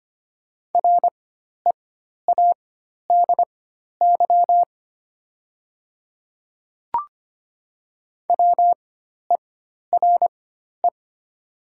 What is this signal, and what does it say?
Morse code, heard as a keyed tone, about 25 wpm, spaced as for 12 wpm, 700 Hz. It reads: READY WERE
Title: 4x Character Spacing - Top 500 Words - 25wpm